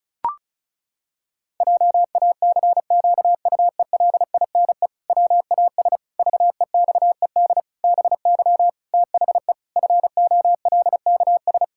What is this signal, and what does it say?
Morse code, heard as a keyed tone, about 36 wpm, 700 Hz. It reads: JACQUELINE WAS VEXED BY THE FOLKS
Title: Pangram Challenge v2 - 35wpm